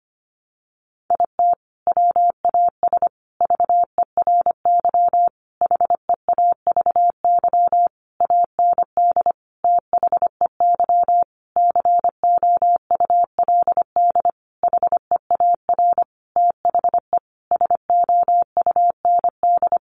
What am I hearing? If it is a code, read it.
Morse code, heard as a keyed tone, about 25 wpm, 700 Hz. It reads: IT WAS VERY HEAVY AND THEY COULD HEAR THE SOUND
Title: Sentences from Top 1100 Words 25wpm